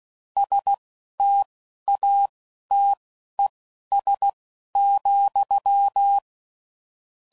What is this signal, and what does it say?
Morse code, heard as a keyed tone, about 16 wpm, spaced as for 11 wpm, 800 Hz. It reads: STATES,
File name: CW Code Groups @10WPM / Technical Subjects / AND Gate